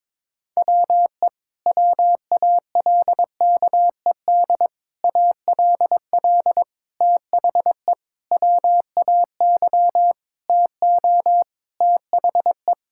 Morse code, heard as a keyed tone, about 22 wpm, 700 Hz. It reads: WE WALKED ALL THE WAY TO THE
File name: practice / 22wpm / Sentences with 4-Word Phrases - 22wpm